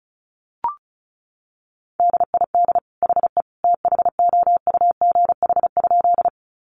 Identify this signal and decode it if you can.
Morse code, heard as a keyed tone, about 34 wpm, 700 Hz. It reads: DID HE THOUGH?
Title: Sentences from Top 400 Words 35wpm